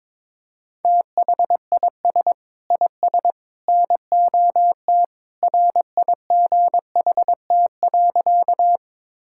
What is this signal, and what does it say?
Morse code, heard as a keyed tone, about 21 wpm, 700 Hz. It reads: THIS IS NOT RIGHT.
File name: practice / 22wpm / Sentences with 3-Word Phrases - 22wpm